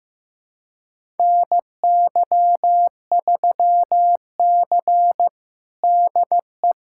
Morse code, heard as a keyed tone, about 15 wpm, 700 Hz. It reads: NY3C DE
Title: USA Callsign DE Callsign - 4 Characters - 15wpm